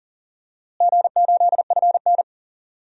Morse code, heard as a keyed tone, about 39 wpm, 700 Hz. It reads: G8FN
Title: DXCC-100 Call Signs - 4 Characters - Repeated 3x - 40wpm